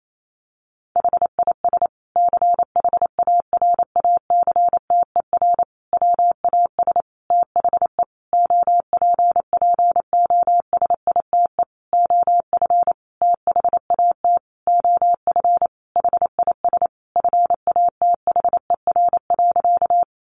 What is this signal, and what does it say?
Morse code, heard as a keyed tone, about 28 wpm, 700 Hz. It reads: HIS CHARACTER WAS THE OPPOSITE OF THAT OF HIS FATHER.
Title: Sentences from Top 1300 Words 28wpm